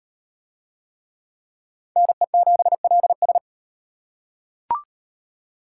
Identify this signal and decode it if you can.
Morse code, heard as a keyed tone, about 38 wpm, 700 Hz. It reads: NE7LS